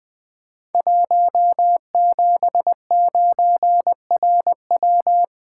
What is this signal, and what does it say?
Morse code, heard as a keyed tone, about 20 wpm, 700 Hz. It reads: 179RW
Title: Code Groups - 5 Letter:Numbers - 20wpm